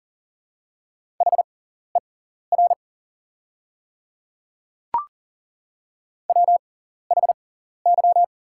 Morse code, heard as a keyed tone, about 40 wpm, spaced as for 14 wpm, 690 Hz. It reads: HER WHY
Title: 6x Character Spacing - Top 500 Words - 40wpm